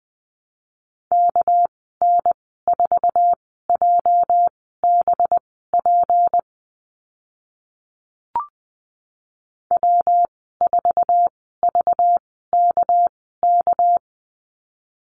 Morse code, heard as a keyed tone, about 20 wpm, spaced as for 14 wpm, 700 Hz. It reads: KN4JBP W4VKK
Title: 207 Lesson 33g - Letter j - Call Signs 20wpm